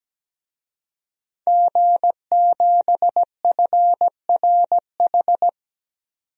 Morse code, heard as a keyed tone, about 17 wpm, 700 Hz. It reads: G7FRH
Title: DXCC-100 Call Signs - 5 Characters - 17wpm